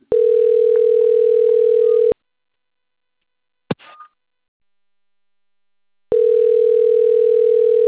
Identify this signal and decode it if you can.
Ringback tone (North America), 440 + 480 Hz.